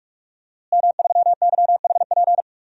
Morse code, heard as a keyed tone, about 43 wpm, 700 Hz. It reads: M3YHP
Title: DXCC-100 Call Signs - 5 Characters - Repeated 2x - 45wpm